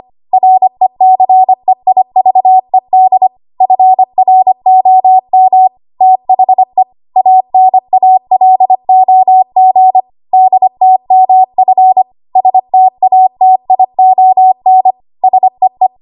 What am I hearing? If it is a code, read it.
Morse code, heard as a keyed tone, about 25 wpm, 750 Hz. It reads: RECEIVED FROM THE ANALOG DTMF STATION SEE
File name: Morse / 25WPM